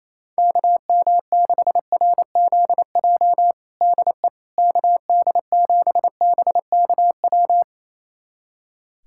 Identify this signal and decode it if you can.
Morse code, heard as a keyed tone, about 27 wpm, 700 Hz. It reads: KM6RZJ DE KD7BKW
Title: USA Callsign DE Callsign - 6 Characters - 28wpm